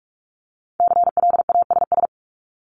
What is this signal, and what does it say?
Morse code, heard as a keyed tone, about 46 wpm, 690 Hz. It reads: CLASS
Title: Multi-letter Phonemes 45wpm